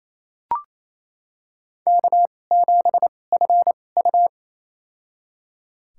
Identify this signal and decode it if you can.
Morse code, heard as a keyed tone, about 28 wpm, spaced as for 20 wpm, 700 Hz. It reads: K7FU
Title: USA Call Signs - 4 Characters - 2x Character Spacing - 28wpm